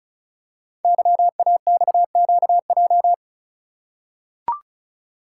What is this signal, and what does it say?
Morse code, heard as a keyed tone, about 34 wpm, 700 Hz. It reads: YAXQJ
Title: Code Groups - 5 Letters - 35wpm